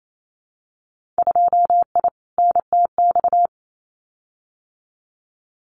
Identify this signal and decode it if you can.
Morse code, heard as a keyed tone, about 28 wpm, 700 Hz. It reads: 2I NTX
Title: Winter Field Day Exchanges Encore 28wpm